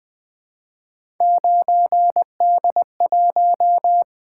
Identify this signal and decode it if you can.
Morse code, heard as a keyed tone, about 20 wpm, 700 Hz. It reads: 9D1
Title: Three Letter-Number Combinations - 20wpm